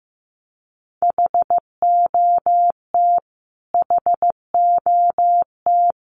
Morse code, heard as a keyed tone, about 15 wpm, 700 Hz. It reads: HOT HOT
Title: Top 300 Words Repeated 15wpm